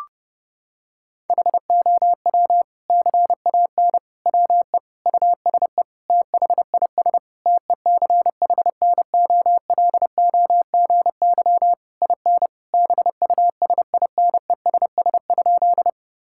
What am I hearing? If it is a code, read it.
Morse code, heard as a keyed tone, about 30 wpm, 700 Hz. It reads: HOW CAN WE USE THIS TECHNOLOGY IN BUSINESS?
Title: Sentences from Top 1000 Words - Review Part 5 - 30wpm